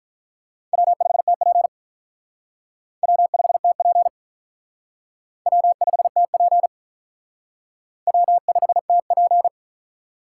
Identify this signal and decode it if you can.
Morse code, heard as a keyed tone, about 43 wpm, 700 Hz. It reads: W5TP W5TP W5TP W5TP
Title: Speed Racing - USA Call Signs 35wpm